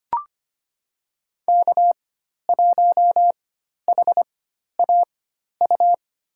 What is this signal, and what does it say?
Morse code, heard as a keyed tone, about 25 wpm, spaced as for 12 wpm, 700 Hz. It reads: K1HAU
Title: USA Call Signs - 5 Characters - 4x Character Spacing - 25wpm